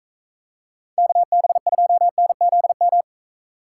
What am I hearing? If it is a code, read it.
Morse code, heard as a keyed tone, about 43 wpm, 700 Hz. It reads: KB2NZM